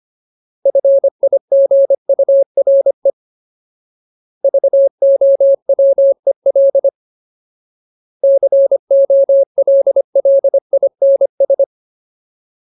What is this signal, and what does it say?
Morse code, heard as a keyed tone, about 25 wpm, 550 Hz. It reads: FIGURE VOWEL COLLINS
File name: Ultimate Word List / 25WPM Ultimate